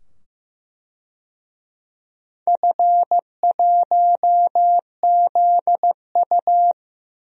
Morse code, heard as a keyed tone, about 15 wpm, 700 Hz. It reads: F1ZU
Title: DXCC-100 Call Signs - 4 Characters - 15wpm